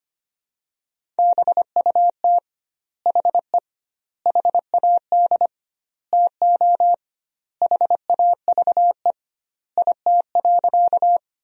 Morse code, heard as a keyed tone, about 25 wpm, 700 Hz. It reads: BUT HE HAD TO HAVE IT.
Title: 2x Word-Spacing - Sentences from Top 100 Words - 25wpm